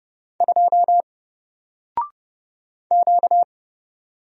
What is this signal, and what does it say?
Morse code, heard as a keyed tone, about 30 wpm, 700 Hz. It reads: 2 Q
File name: Single-Letter-Number-30wpm